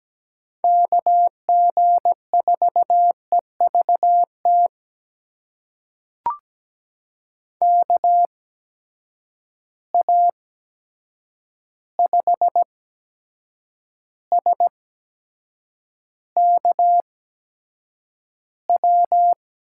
Morse code, heard as a keyed tone, about 17 wpm, spaced as for 7 wpm, 700 Hz. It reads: KG4EVT KA5SKW
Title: USA Call Signs - 6 Characters - 8x Character Spacing - 17wpm